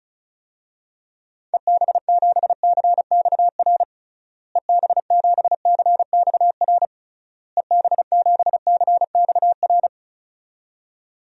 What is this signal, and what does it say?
Morse code, heard as a keyed tone, about 36 wpm, 700 Hz. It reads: EB7CXR EB7CXR EB7CXR
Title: DXCC-100 Call Signs - 6 Characters - Repeated 3x - 35wpm